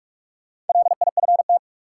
Morse code, heard as a keyed tone, about 42 wpm, 700 Hz. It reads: LIFT
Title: Sentences from Top 1000 Words - New Word Drill - 45wpm